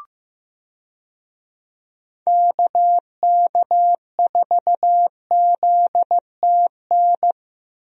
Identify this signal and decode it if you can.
Morse code, heard as a keyed tone, about 16 wpm, 700 Hz. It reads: KK4ZTN
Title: USA Call Signs - 6 Characters - 15wpm